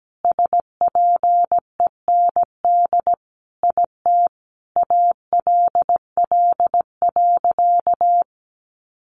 Morse code, heard as a keyed tone, about 17 wpm, 700 Hz. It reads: SPEND IT ALL.
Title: Sentences from Top 800 Words 17wpm